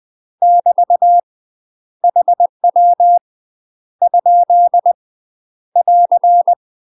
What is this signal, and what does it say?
Morse code, heard as a keyed tone, about 20 wpm, 700 Hz. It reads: <BT> HW ? <AR>
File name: UK MASTER QSOs TRAINING / 20WPM / UK MASTER QSOs 0022